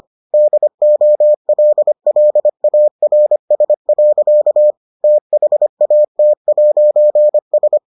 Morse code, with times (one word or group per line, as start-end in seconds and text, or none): 0.3-4.7 DOLLARS.
5.0-7.8 THAT'S